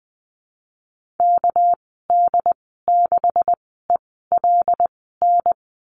Morse code, heard as a keyed tone, about 20 wpm, spaced as for 14 wpm, 700 Hz. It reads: KD6ELN
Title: 213 Lesson 34g - Number 8 - Call Signs 20wpm